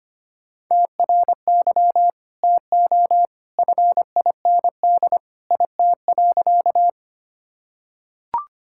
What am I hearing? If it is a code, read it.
Morse code, heard as a keyed tone, about 25 wpm, 700 Hz. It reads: TRY TO FIND IT.
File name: practice / 25wpm / Sentences with 2-Word Phrases - 25wpm